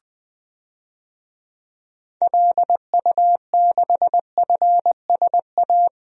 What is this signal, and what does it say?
Morse code, heard as a keyed tone, about 20 wpm, 700 Hz. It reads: LU6FSA
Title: DXCC-100 Call Signs - 6 Characters - Repeated 3x - 20wpm